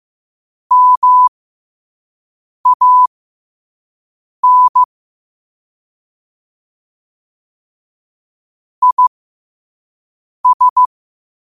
Morse code, heard as a keyed tone, about 15 wpm, spaced as for 5 wpm, 1000 Hz. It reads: MAN IS
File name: qotdfiles / 5wpm